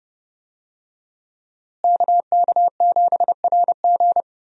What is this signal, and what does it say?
Morse code, heard as a keyed tone, about 28 wpm, 700 Hz. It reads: KK7RG
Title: USA Call Signs - 5 Characters - Repeated 3x - 30wpm